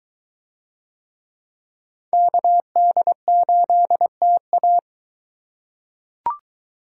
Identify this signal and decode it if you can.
Morse code, heard as a keyed tone, about 22 wpm, 700 Hz. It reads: KD8TA